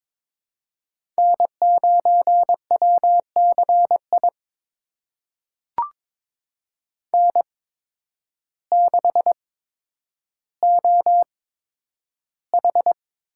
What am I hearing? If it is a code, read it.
Morse code, heard as a keyed tone, about 22 wpm, spaced as for 10 wpm, 700 Hz. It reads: N9WCI N6OH
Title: USA Call Signs - 8x Character Spacing - 22wpm